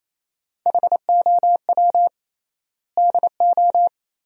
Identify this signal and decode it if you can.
Morse code, heard as a keyed tone, about 28 wpm, 700 Hz. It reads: HOW DO
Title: 3x Word-Spacing - Sentences from Top 400 Words - 28wpm